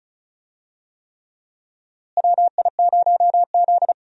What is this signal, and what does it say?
Morse code, heard as a keyed tone, about 36 wpm, 700 Hz. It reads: WI0Z
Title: USA Call Signs - 4 Characters - Repeated 3x - 35wpm